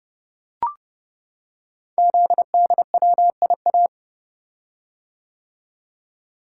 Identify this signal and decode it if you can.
Morse code, heard as a keyed tone, about 30 wpm, 700 Hz. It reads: ZDWIA
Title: Code Groups - 5 Letters - 30wpm